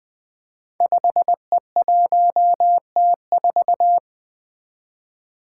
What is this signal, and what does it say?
Morse code, heard as a keyed tone, about 20 wpm, 700 Hz. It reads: 5E1T4
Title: Five Challenging Letter-Number Combinations 20wpm